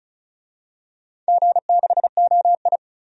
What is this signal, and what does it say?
Morse code, heard as a keyed tone, about 36 wpm, 700 Hz. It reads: G6OI
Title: DXCC-100 Call Signs - 4 Characters - 35wpm